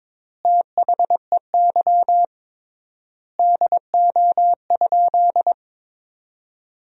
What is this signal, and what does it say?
Morse code, heard as a keyed tone, about 22 wpm, 700 Hz. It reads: THEY DO?
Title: 3x Word-Spacing - Sentences from Top 100 Words - 22wpm